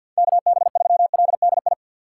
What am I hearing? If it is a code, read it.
Morse code, heard as a keyed tone, about 51 wpm, 700 Hz. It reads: KB3LDI
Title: USA Call Signs - 4x Character Spacing - 50wpm